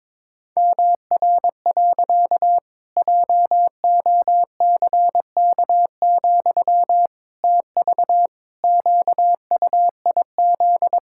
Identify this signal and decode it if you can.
Morse code, heard as a keyed tone, about 22 wpm, 700 Hz. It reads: MR. JOCK, TV QUIZ
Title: Pangram Challenge v2 - 22wpm